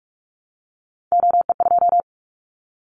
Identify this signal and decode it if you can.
Morse code, heard as a keyed tone, about 46 wpm, 700 Hz. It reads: OE2
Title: Callsign Prefixes for Top 100 DXCC - 45wpm